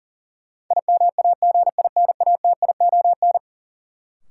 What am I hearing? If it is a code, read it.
Morse code, heard as a keyed tone, about 40 wpm, 700 Hz. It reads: IMAGINATION
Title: Sentences from Top 1300 Words - New Word Drill Alt - 40wpm